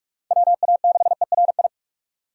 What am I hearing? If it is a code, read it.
Morse code, heard as a keyed tone, about 39 wpm, 700 Hz. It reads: WA6ERI